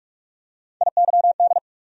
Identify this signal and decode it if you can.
Morse code, heard as a keyed tone, about 46 wpm, 700 Hz. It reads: IYD
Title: USA Call Sign Postfixes - 3 Letters 45wpm